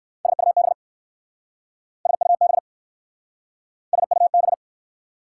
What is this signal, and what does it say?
Morse code, heard as a keyed tone, about 60 wpm, 690 Hz. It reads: SUB SUB SUB